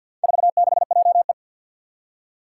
Morse code, heard as a keyed tone, about 50 wpm, 700 Hz. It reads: 46JE
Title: Four Challenging Letter-Number Combinations 50wpm